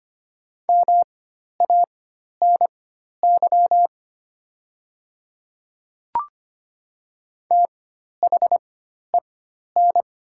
Morse code, heard as a keyed tone, about 25 wpm, spaced as for 12 wpm, 700 Hz. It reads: MANY THEN
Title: 4x Character Spacing - Top 300 Words - 25wpm